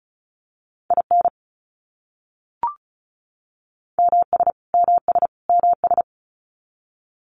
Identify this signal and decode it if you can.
Morse code, heard as a keyed tone, about 34 wpm, 700 Hz. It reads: IN MS MS MS
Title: US States Repeated 3x 35wpm